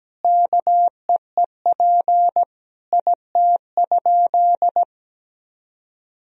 Morse code, heard as a keyed tone, about 17 wpm, 700 Hz. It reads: KEEP IT?
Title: Sentences from Top 300 Words and Abbreviations 17wpm